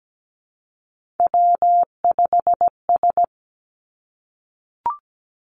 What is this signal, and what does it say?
Morse code, heard as a keyed tone, about 17 wpm, 700 Hz. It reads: W5S
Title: Three Challenging Letter-Number Combinations 17wpm